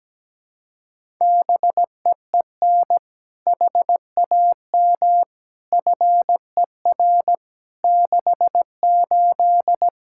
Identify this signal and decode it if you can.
Morse code, heard as a keyed tone, about 17 wpm, 700 Hz. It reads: BEEN HAM FER 68
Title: Practice QSO - Exchange 3 - More Info - 17wpm